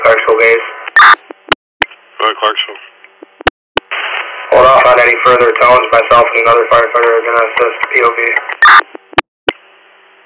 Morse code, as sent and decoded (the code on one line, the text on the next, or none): none